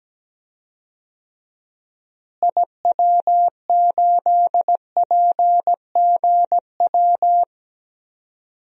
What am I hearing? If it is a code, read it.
Morse code, heard as a keyed tone, about 17 wpm, 700 Hz. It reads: IW8PGW
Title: DXCC-100 Call Signs - 6 Characters - Repeated 2x - 17wpm